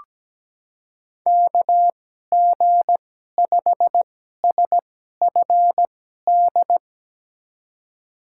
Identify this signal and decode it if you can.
Morse code, heard as a keyed tone, about 17 wpm, spaced as for 12 wpm, 700 Hz. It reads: KG5SFD
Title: USA Call Signs - 6 Characters - 2x Character Spacing - 17wpm